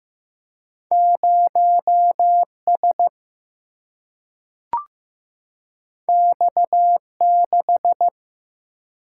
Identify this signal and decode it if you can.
Morse code, heard as a keyed tone, about 15 wpm, 700 Hz. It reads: 0S X6
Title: Single-Letter Single-Number 15wpm